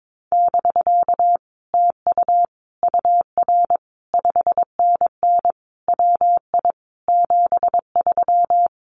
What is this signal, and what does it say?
Morse code, heard as a keyed tone, about 22 wpm, 700 Hz. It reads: <BK> TU UR 5NN WI 73
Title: Common QSO Phrases Encore 22wpm